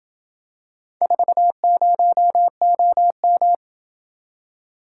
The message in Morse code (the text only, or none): 40OM